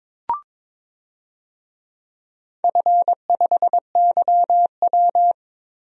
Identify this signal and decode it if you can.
Morse code, heard as a keyed tone, about 22 wpm, 700 Hz. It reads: F5YW
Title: DXCC-100 Call Signs - 4 Characters - 22wpm